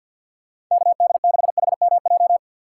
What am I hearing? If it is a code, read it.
Morse code, heard as a keyed tone, about 47 wpm, 700 Hz. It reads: KD6HMJ
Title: USA Call Signs - 6 Characters - 6x Character Spacing - 50wpm